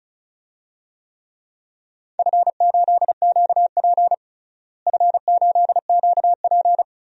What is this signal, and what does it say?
Morse code, heard as a keyed tone, about 34 wpm, 700 Hz. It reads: F8QP F8QP
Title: DXCC-100 Call Signs - 4 Characters - Repeated 3x - 35wpm